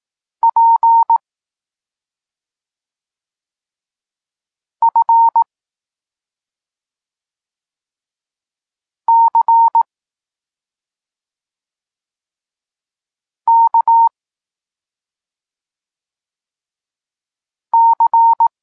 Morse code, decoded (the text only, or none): P F C K C